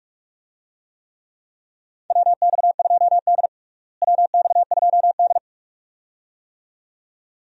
Morse code, heard as a keyed tone, about 46 wpm, 700 Hz. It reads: WX2D WX2D
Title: USA Call Signs - 4 Characters - Repeated 2x - 45wpm